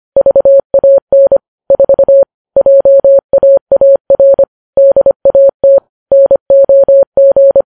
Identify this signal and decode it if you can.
Morse code, heard as a keyed tone, about 25 wpm, 550 Hz. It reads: VAN 4 JAAR DAT NOG